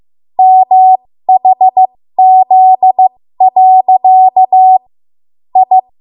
Morse code, heard as a keyed tone, about 15 wpm, spaced as for 13 wpm, 750 Hz. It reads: MHZ. I
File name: Morse / 13WPM